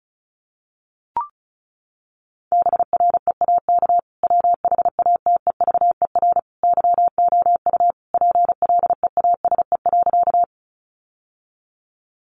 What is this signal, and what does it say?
Morse code, heard as a keyed tone, about 34 wpm, 700 Hz. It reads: BREAK WHATEVER YOU PLEASE.
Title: Sentences from Top 1500 Words 35wpm